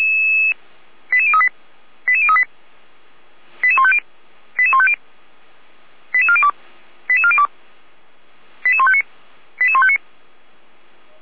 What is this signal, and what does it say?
Five-tone selective call (ZVEI), heard as quick runs of five tones, tones of about 70 ms. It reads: 88038 88038 88150 88150 88402 88402 88159 88159